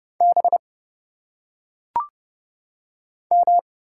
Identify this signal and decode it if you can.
Morse code, heard as a keyed tone, about 30 wpm, 700 Hz. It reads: B M
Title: Single Letters - Phonetic Pronunciation 30wpm